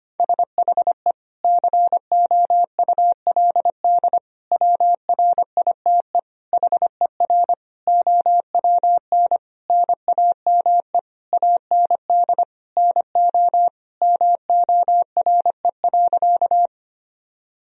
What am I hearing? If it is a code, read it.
Morse code, heard as a keyed tone, about 25 wpm, 700 Hz. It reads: SHE COULD WRITE HER OWN NAME AND NO MORE.
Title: Sentences from Top 1000 Words - Review Part 1 - 25wpm